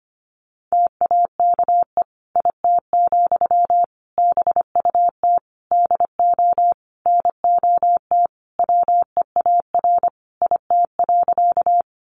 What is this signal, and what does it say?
Morse code, heard as a keyed tone, about 25 wpm, 700 Hz. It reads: TAKE IT, BUT DO NOT WEAR IT.
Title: Sentences from Top 1500 Words 25wpm